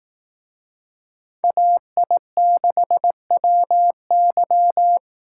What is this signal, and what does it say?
Morse code, heard as a keyed tone, about 18 wpm, 700 Hz. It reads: AI6WY
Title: Speed Racing - USA 2x2 Call Signs 15wpm